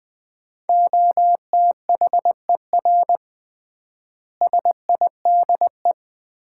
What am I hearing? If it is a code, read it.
Morse code, heard as a keyed tone, about 20 wpm, 700 Hz. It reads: OTHER SIDE
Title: 3x Word-Spacing - Sets of 4 Words - 20wpm